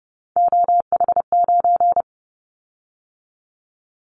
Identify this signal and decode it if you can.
Morse code, heard as a keyed tone, about 30 wpm, 700 Hz. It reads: OH9
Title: Callsign Prefixes for Top 50 DXCC - 30wpm